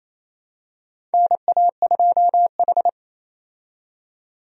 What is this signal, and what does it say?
Morse code, heard as a keyed tone, about 28 wpm, 700 Hz. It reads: NA2H
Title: USA Call Signs - 4 Characters - 28wpm